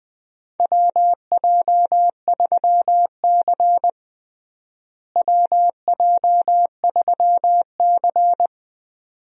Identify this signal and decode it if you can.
Morse code, heard as a keyed tone, about 20 wpm, 700 Hz. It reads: WJ3C WJ3C
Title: USA Call Signs - 4 Characters - Repeated 2x - 20wpm